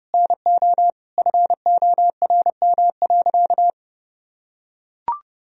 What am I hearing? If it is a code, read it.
Morse code, heard as a keyed tone, about 30 wpm, 700 Hz. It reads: NO FORM.
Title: Sentences from Top 1000 Words - Review Part 1 - 30wpm